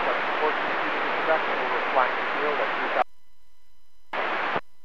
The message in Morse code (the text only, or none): none